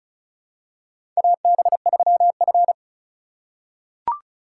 Morse code, heard as a keyed tone, about 34 wpm, 700 Hz. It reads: AB3F
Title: USA Call Signs - 4 Characters - 4x Character Spacing - 35wpm